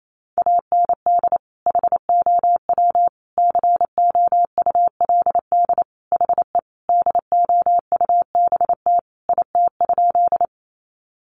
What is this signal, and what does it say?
Morse code, heard as a keyed tone, about 28 wpm, 700 Hz. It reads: AND HOW COULD HE DOUBT IT?
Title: Sentences from Top 1500 Words 28wpm